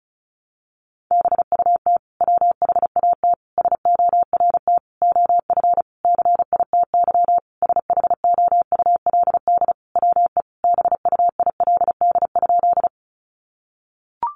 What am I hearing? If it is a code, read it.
Morse code, heard as a keyed tone, about 34 wpm, 700 Hz. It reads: BUT WHAT SORT OF CITY SHOULD WE BUILD?
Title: Sentences from Top 1300 Words 35wpm